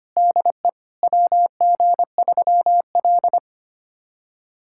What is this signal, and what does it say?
Morse code, heard as a keyed tone, about 25 wpm, 700 Hz. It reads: DE WG3L
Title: USA Callsign DE Callsign - 4 Characters - 25wpm